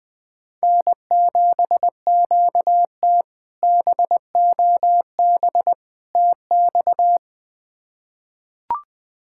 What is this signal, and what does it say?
Morse code, heard as a keyed tone, about 21 wpm, 700 Hz. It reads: N7QT BOB TX
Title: K1USN SST QSO Practice - 20wpm